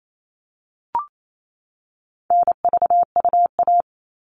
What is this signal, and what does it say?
Morse code, heard as a keyed tone, about 28 wpm, 700 Hz. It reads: NVUA
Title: All Cut Numbers - 4-Digit 28wpm